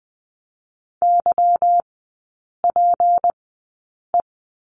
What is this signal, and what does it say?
Morse code, heard as a keyed tone, about 20 wpm, spaced as for 8 wpm, 700 Hz. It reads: YPE
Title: 152 Lesson 25e - Letter p - Three Characters 20wpm